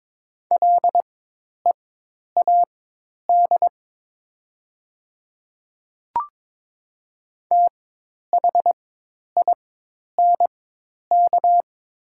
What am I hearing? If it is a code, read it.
Morse code, heard as a keyed tone, about 22 wpm, spaced as for 10 wpm, 700 Hz. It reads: LEAD THINK
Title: 4x Character Spacing - Top 500 Words - 22wpm